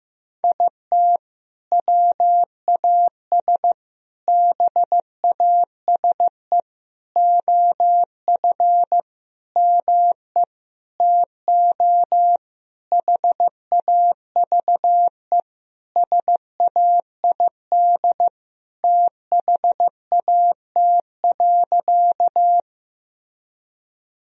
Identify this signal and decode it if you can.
Morse code, heard as a keyed tone, about 15 wpm, 700 Hz. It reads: IT WAS BASE OF ME TO HAVE SAID THAT.
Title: Sentences from Top 1000 Words - Review Part 1 - 15wpm